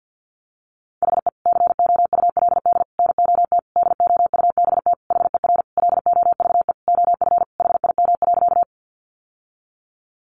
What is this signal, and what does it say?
Morse code, heard as a keyed tone, about 50 wpm, 700 Hz. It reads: HE COULD NOT DOUBT HER LOVE OF HIM.
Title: Sentences from Top 1500 Words 50wpm